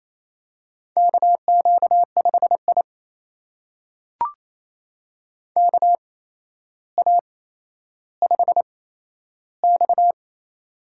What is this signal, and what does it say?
Morse code, heard as a keyed tone, about 27 wpm, spaced as for 12 wpm, 700 Hz. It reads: KQ5I KA5X
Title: USA Call Signs - 4 Characters - 8x Character Spacing - 28wpm